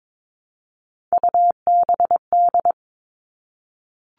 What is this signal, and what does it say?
Morse code, heard as a keyed tone, about 22 wpm, 700 Hz. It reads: UBD
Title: All Cut Numbers - 3-Digit 22wpm